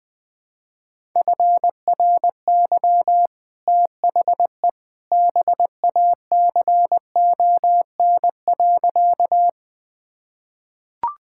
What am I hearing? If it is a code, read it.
Morse code, heard as a keyed tone, about 20 wpm, 700 Hz. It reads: FRY THE BACON.
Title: Easy Sentences with Hints - 20wpm